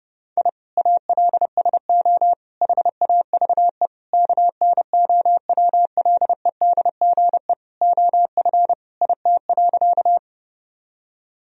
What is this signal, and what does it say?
Morse code, heard as a keyed tone, about 30 wpm, 700 Hz. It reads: I ALSO HAVE KNOWLEDGE OF IT.
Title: Sentences from Top 1000 Words - Review Part 3 - 30wpm